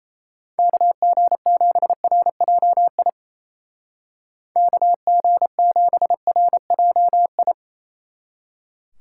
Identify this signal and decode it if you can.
Morse code, heard as a keyed tone, about 33 wpm, 700 Hz. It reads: KG7RJI KG7RJI
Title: Speed Racing - USA 2x3 Call Signs 28wpm